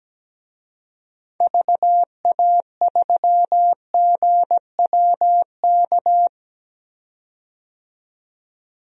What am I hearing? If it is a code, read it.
Morse code, heard as a keyed tone, about 17 wpm, 700 Hz. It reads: VA3GWK
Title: DXCC-100 Call Signs - 6 Characters - 17wpm